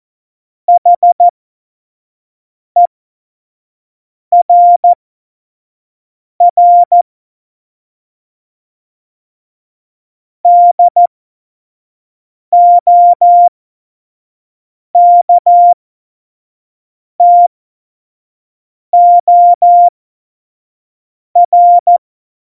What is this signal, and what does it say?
Morse code, heard as a keyed tone, about 14 wpm, spaced as for 5 wpm, 700 Hz. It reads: HERR DOKTOR
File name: Morsekurs / Witze / Witz 05. Dezember 2017, 25 F70